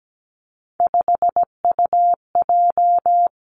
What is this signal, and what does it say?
Morse code, heard as a keyed tone, about 17 wpm, 700 Hz. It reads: 5UJ
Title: Three Challenging Letter-Number Combinations 17wpm